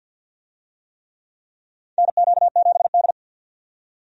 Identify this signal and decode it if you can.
Morse code, heard as a keyed tone, about 49 wpm, 700 Hz. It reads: NQ7D
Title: USA Call Signs - 4 Characters - 50wpm